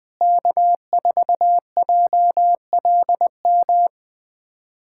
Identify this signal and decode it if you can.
Morse code, heard as a keyed tone, about 21 wpm, 700 Hz. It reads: K4JLM
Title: USA Callsign DE Callsign - 5 Characters - 20wpm